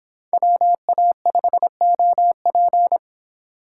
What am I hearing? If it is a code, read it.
Morse code, heard as a keyed tone, about 24 wpm, 700 Hz. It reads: WA5OP